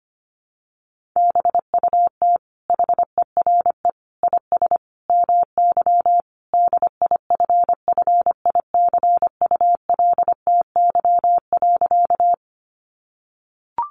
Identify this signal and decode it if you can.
Morse code, heard as a keyed tone, about 25 wpm, 700 Hz. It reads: BUT HERE IS MY DIFFICULTY.
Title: Sentences from Top 1500 Words 25wpm